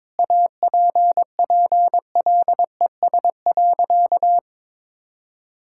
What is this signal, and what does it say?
Morse code, heard as a keyed tone, about 22 wpm, 700 Hz. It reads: APPLES.
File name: Difficult Sentences with Hints - 22wpm